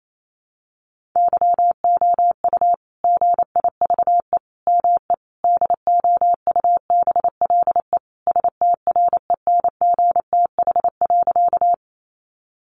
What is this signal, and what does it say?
Morse code, heard as a keyed tone, about 28 wpm, 700 Hz. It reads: YOU GIVE ME DOUBLE STRENGTH.
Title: Sentences from Top 1400 Words 28wpm